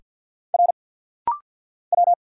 Morse code, heard as a keyed tone, about 50 wpm, 700 Hz. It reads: R W
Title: Single Letters - Rapid-Fire 50wpm